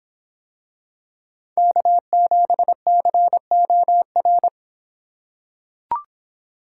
Morse code, heard as a keyed tone, about 25 wpm, 700 Hz. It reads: K7COR